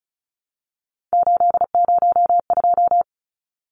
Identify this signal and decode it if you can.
Morse code, heard as a keyed tone, about 35 wpm, 700 Hz. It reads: 802
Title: Numbers - 3-Digit 35wpm